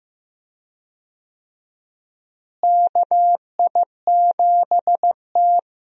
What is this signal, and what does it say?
Morse code, heard as a keyed tone, about 15 wpm, 700 Hz. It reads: KI7T
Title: USA Call Signs - 4 Characters - Repeated 3x - 15wpm